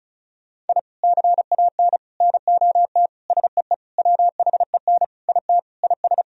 Morse code, heard as a keyed tone, about 34 wpm, 700 Hz. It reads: I CAN NOT SEE WHEN IT IS
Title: Sentences from Top 1000 Words - Review Part 4 - 35wpm